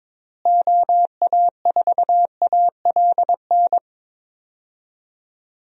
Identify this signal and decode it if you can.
Morse code, heard as a keyed tone, about 22 wpm, 700 Hz. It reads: OA4ALN
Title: USA Call Signs - 6 Characters - Repeated 2x - 22wpm